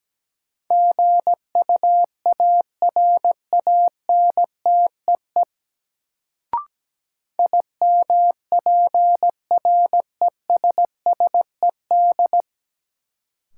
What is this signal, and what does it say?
Morse code, heard as a keyed tone, about 17 wpm, 700 Hz. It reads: GUARANTEE IMPRESSED
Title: Sentences from Top 1600 Words - New Word Drill - 17wpm